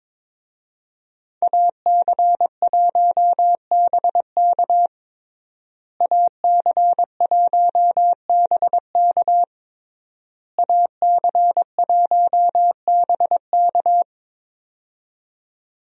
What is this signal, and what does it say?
Morse code, heard as a keyed tone, about 22 wpm, 700 Hz. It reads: AC1BK AC1BK AC1BK
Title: USA Call Signs - 5 Characters - Repeated 3x - 22wpm